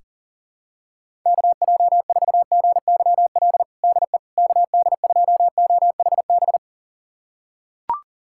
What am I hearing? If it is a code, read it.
Morse code, heard as a keyed tone, about 37 wpm, 700 Hz. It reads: KJ4GYL DE KD2OHB
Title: USA Callsign DE Callsign - 6 Characters - 40wpm